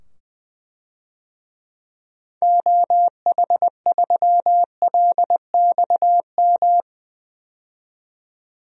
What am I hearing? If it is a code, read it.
Morse code, heard as a keyed tone, about 20 wpm, 700 Hz. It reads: OH3LXM
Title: DXCC-100 Call Signs - 6 Characters - 20wpm